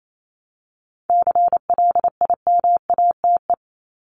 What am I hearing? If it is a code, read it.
Morse code, heard as a keyed tone, about 28 wpm, 700 Hz. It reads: CLIMATE
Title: Top 1501-1750 Words 28wpm